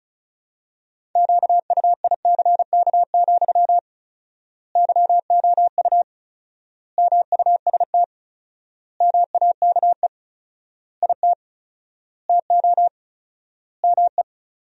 Morse code, heard as a keyed tone, about 34 wpm, 700 Hz. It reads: QUICK, YOU MUST MAKE IT TO ME
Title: 4x Word-Spacing - Sentences from Top 500 Words - 35wpm